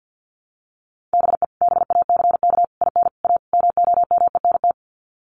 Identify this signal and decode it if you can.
Morse code, heard as a keyed tone, about 50 wpm, 700 Hz. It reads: BE BACK IN A MOMENT
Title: Sets of 5 Words 50wpm